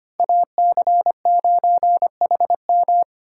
Morse code, heard as a keyed tone, about 25 wpm, 700 Hz. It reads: AC9HM
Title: USA Call Signs - 2x Character Spacing - 25wpm